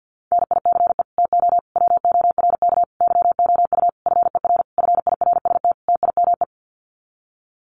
Morse code, heard as a keyed tone, about 50 wpm, 700 Hz. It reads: NICE TO WORK YOU FER FIRST TIME
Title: Common QSO Phrases 50wpm